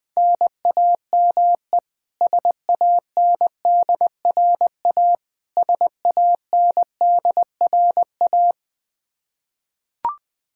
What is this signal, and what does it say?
Morse code, heard as a keyed tone, about 20 wpm, 700 Hz. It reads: NAME SANDRA SANDRA
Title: Practice QSO - Exchange 1 - Beginning - 20wpm